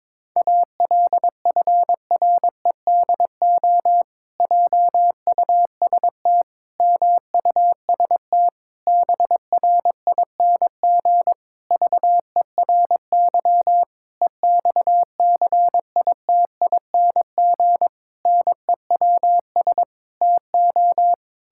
Morse code, heard as a keyed tone, about 22 wpm, 700 Hz. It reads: ALFREDO JUST MUST BRING VERY EXCITING NEWS TO
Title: Pangram Challenge v2 - 22wpm